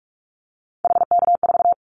Morse code, heard as a keyed tone, about 44 wpm, 700 Hz. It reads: HK4
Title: Callsign Prefixes for Top 100 DXCC - 45wpm